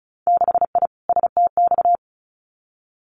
Morse code, heard as a keyed tone, about 36 wpm, 700 Hz. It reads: 6I STX
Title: Winter Field Day Exchanges Encore 35wpm